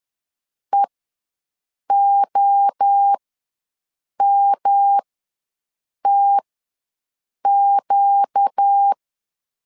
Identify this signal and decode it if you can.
Morse code, heard as a keyed tone, about 10 wpm, spaced as for 6 wpm, 800 Hz. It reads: EOMTQ
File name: LEZIONI / 5 LBVC / 5 LB ws 10wpm 3616